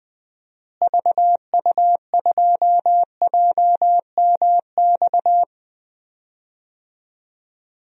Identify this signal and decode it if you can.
Morse code, heard as a keyed tone, about 20 wpm, 700 Hz. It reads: VU2JMX
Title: DXCC-100 Call Signs - 6 Characters - Repeated 3x - 20wpm